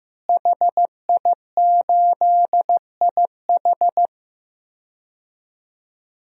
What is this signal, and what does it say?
Morse code, heard as a keyed tone, about 15 wpm, 700 Hz. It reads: HI8IH
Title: DXCC-100 Call Signs - 5 Characters - Repeated 2x - 15wpm